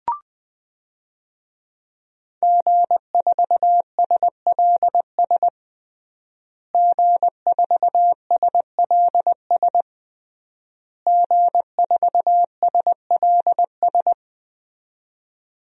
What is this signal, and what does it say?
Morse code, heard as a keyed tone, about 21 wpm, 700 Hz. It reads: G4SLS G4SLS G4SLS
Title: DXCC-100 Call Signs - 5 Characters - Repeated 3x - 20wpm